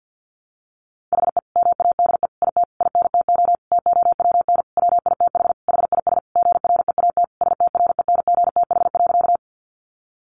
Morse code, heard as a keyed tone, about 50 wpm, 700 Hz. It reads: HE MADE IT INTO TOWN WITH HIS GREAT STRENGTH.
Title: Sentences from Top 1300 Words 50wpm